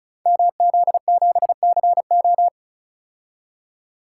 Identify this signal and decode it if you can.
Morse code, heard as a keyed tone, about 36 wpm, 700 Hz. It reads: MZ7CO
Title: Code Groups - 5 Letter:Numbers - 35wpm